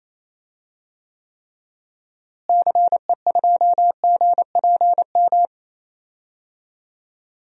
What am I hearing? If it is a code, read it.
Morse code, heard as a keyed tone, about 28 wpm, 700 Hz. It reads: CE2GPM
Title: DXCC-100 Call Signs - 6 Characters - Repeated 3x - 28wpm